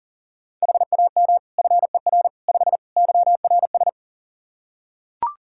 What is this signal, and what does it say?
Morse code, heard as a keyed tone, about 40 wpm, 700 Hz. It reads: HAM FER 5 YRS
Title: K1USN SST QSO Practice 40wpm